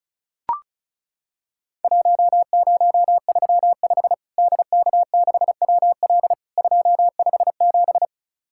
Morse code, heard as a keyed tone, about 35 wpm, 700 Hz. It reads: 1035 DK6WL 257
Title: Worked All Europe DX Contest - QTC Exchange - 35wpm